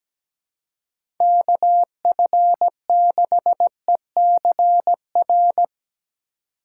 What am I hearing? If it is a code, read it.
Morse code, heard as a keyed tone, about 17 wpm, 700 Hz. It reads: KF6ECR
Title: USA Call Signs - 6 Characters - 3x Character Spacing - 17wpm